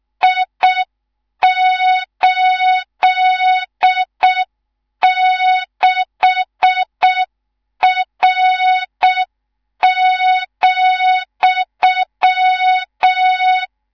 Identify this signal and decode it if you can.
Morse code, heard as a keyed tone, about 6 wpm, 3700 Hz. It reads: I86R,
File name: Morze-HA2MN / hang / 19 GYAKORLAT